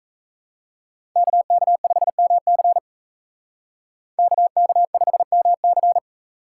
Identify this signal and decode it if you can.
Morse code, heard as a keyed tone, about 37 wpm, 700 Hz. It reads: KK5MC KK5MC